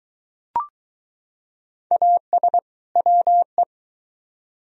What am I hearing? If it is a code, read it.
Morse code, heard as a keyed tone, about 21 wpm, 700 Hz. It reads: AS WE